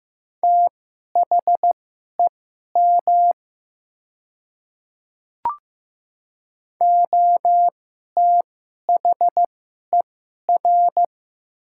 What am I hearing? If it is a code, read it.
Morse code, heard as a keyed tone, about 16 wpm, spaced as for 11 wpm, 700 Hz. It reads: THEM OTHER
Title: 2x Character Spacing - Top 100 Words - 15wpm